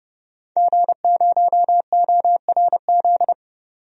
Morse code, heard as a keyed tone, about 30 wpm, 700 Hz. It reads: G0ORZ
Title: DXCC-100 Call Signs - 5 Characters - 30wpm